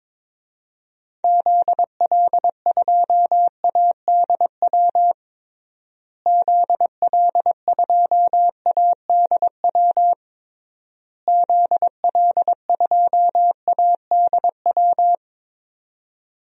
Morse code, heard as a keyed tone, about 22 wpm, 700 Hz. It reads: ZL2ADW ZL2ADW ZL2ADW
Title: DXCC-100 Call Signs - 6 Characters - Repeated 3x - 22wpm